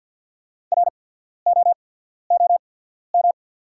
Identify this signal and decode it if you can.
Morse code, heard as a keyed tone, about 48 wpm, spaced as for 14 wpm, 690 Hz. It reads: ROOM
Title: 8x Character Spacing - Top 300 Words - 50wpm